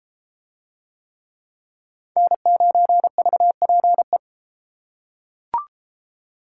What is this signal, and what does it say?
Morse code, heard as a keyed tone, about 32 wpm, 700 Hz. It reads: N9VPE